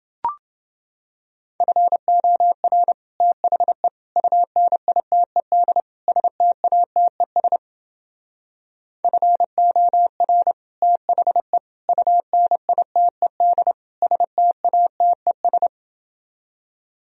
Morse code, heard as a keyed tone, about 30 wpm, 700 Hz. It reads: FOR THE UNITED STATES FOR THE UNITED STATES